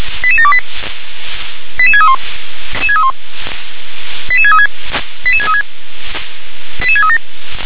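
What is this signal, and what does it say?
Five-tone selective call (ZVEI), heard as quick runs of five tones, tones of about 70 ms. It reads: 88728 88631 88636 88636 88637
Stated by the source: ZVEI five-tone selective call received by SDR.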